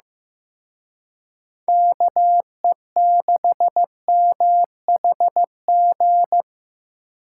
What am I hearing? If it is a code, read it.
Morse code, heard as a keyed tone, about 18 wpm, 700 Hz. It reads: KE6MHG
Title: Speed Racing - USA 2x3 Call Signs 15wpm